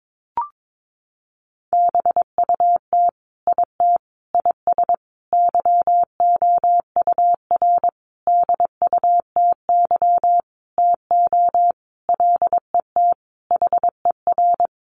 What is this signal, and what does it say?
Morse code, heard as a keyed tone, about 22 wpm, 700 Hz. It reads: BUT IT IS YOUR DUTY TO LET HER
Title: Sentences from Top 1500 Words 22wpm